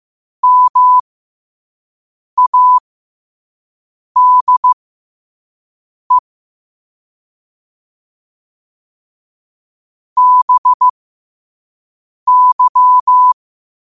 Morse code, heard as a keyed tone, about 15 wpm, spaced as for 5 wpm, 1000 Hz. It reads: MADE BY
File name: qotdfiles / 5wpm